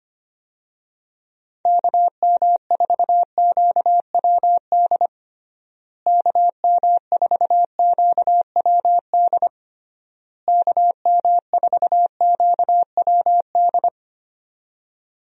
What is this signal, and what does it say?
Morse code, heard as a keyed tone, about 25 wpm, 700 Hz. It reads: KM4QWD KM4QWD KM4QWD
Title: USA Call Signs - 6 Characters - Repeated 3x - 25wpm